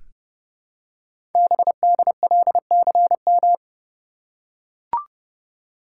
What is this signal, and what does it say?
Morse code, heard as a keyed tone, about 30 wpm, 700 Hz. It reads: BDLCM
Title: Code Groups - 5 Letters - 30wpm